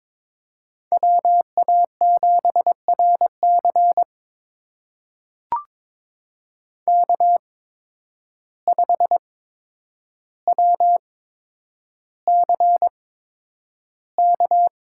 Morse code, heard as a keyed tone, about 22 wpm, spaced as for 9 wpm, 700 Hz. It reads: WA7RC K5WCK
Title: USA Call Signs - 5 Characters - 8x Character Spacing - 22wpm